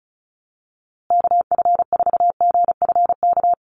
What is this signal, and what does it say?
Morse code, heard as a keyed tone, about 34 wpm, 700 Hz. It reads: KF4GFK
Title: USA Call Signs Repeated 2x 35wpm